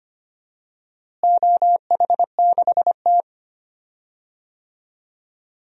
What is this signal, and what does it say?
Morse code, heard as a keyed tone, about 25 wpm, 700 Hz. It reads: OH6T
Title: DXCC-100 Call Signs - 4 Characters - Repeated 3x - 25wpm